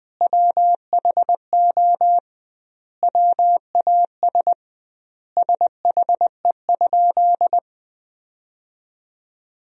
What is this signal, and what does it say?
Morse code, heard as a keyed tone, about 21 wpm, 700 Hz. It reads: WHO WAS SHE?
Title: 2x Word-Spacing - Sentences from Top 100 Words - 20wpm